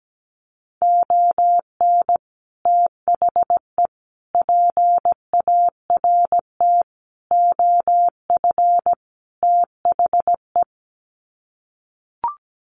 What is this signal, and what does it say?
Morse code, heard as a keyed tone, about 17 wpm, 700 Hz. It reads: ON THE PART OF THE
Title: Sets of 5 Words 17wpm